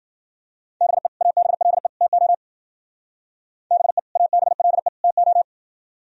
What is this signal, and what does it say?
Morse code, heard as a keyed tone, about 52 wpm, 700 Hz. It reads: BE ABLE TO BE ABLE TO